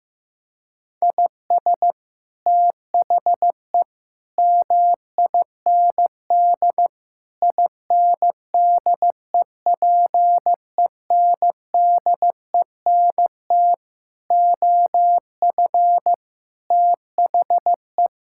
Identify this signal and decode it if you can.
Morse code, heard as a keyed tone, about 16 wpm, 700 Hz. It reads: IS THE MIND INDEPENDENT OF THE
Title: Sentences from Top 1000 Words - Review Part 6 - 15wpm